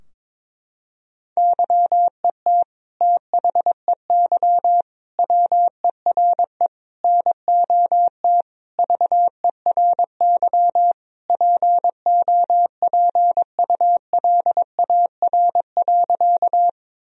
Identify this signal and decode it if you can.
Morse code, heard as a keyed tone, about 21 wpm, 700 Hz. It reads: YET THEY WERE NOT VERY POPULAR.
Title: Sentences from Top 1000 Words - Review Part 3 - 22wpm